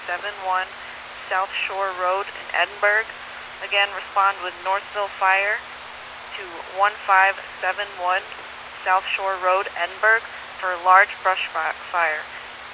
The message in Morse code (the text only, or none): none